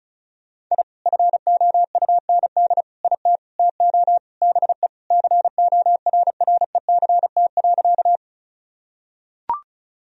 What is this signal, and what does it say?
Morse code, heard as a keyed tone, about 36 wpm, 700 Hz. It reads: I FOUND IT TO BE CORRECT.
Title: Sentences from Top 1000 Words - Review Part 2 - 35wpm